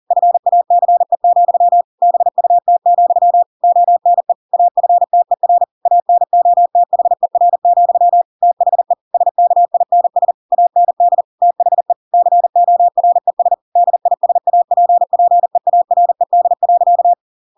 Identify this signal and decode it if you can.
Morse code, heard as a keyed tone, about 40 wpm, 700 Hz. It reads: FACE, BUT, ONE AFTER ANOTHER, THE SKINS AND THE CORES DISAPPEARED.
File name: Pinocchio / Pinocchio ch7 40 40 ik1hge